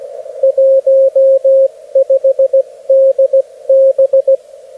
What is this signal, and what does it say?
Morse code, heard as a keyed tone, about 16 wpm, 540 Hz, with radio noise. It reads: 15DB
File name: beacons / HG1BTV